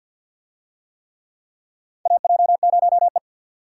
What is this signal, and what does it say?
Morse code, heard as a keyed tone, about 50 wpm, 700 Hz. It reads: AJ0E
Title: USA Call Signs - 4 Characters - 50wpm